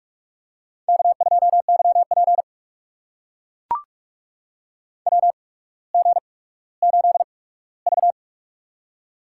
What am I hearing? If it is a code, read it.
Morse code, heard as a keyed tone, about 44 wpm, spaced as for 19 wpm, 700 Hz. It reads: K2YP WG8V
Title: USA Call Signs - 4 Characters - 8x Character Spacing - 45wpm